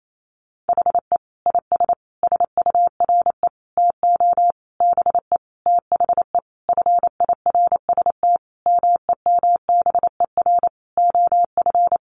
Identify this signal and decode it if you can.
Morse code, heard as a keyed tone, about 28 wpm, 700 Hz. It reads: HE IS SURE TO BE THE FIRST MEMBER OF
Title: Sentences from Top 1400 Words 28wpm